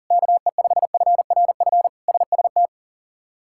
Morse code, heard as a keyed tone, about 40 wpm, 700 Hz. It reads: KE5FRF SST
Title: K1USN SST QSO Practice - 40wpm